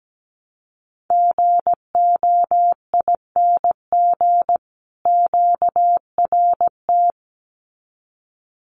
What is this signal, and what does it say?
Morse code, heard as a keyed tone, about 17 wpm, 700 Hz. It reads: GOING QRT
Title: Common QSO Phrases 17wpm